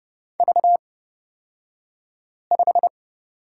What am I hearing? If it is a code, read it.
Morse code, heard as a keyed tone, about 30 wpm, spaced as for 5 wpm, 700 Hz. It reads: V5
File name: Lesson 36d - BT - Two Characters 30wpm